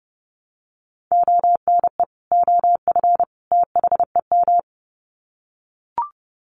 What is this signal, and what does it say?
Morse code, heard as a keyed tone, about 30 wpm, 700 Hz. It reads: ONE OF THEM
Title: Sets of 3 Words 30wpm